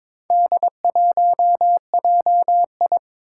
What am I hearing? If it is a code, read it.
Morse code, heard as a keyed tone, about 23 wpm, 700 Hz. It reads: D1JI
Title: Four Challenging Letter-Number Combinations 22wpm